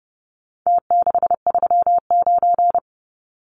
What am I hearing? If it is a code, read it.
Morse code, heard as a keyed tone, about 30 wpm, 700 Hz. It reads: T639
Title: Numbers - Contest Style - 4-Digit 30wpm